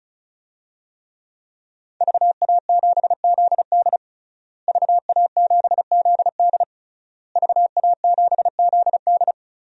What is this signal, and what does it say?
Morse code, heard as a keyed tone, about 36 wpm, 700 Hz. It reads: VA7ZD VA7ZD VA7ZD
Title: DXCC-100 Call Signs - 5 Characters - Repeated 3x - 35wpm